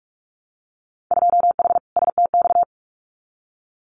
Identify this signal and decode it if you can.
Morse code, heard as a keyed tone, about 46 wpm, 700 Hz. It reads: 2H STX
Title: Winter Field Day Exchanges Encore 45wpm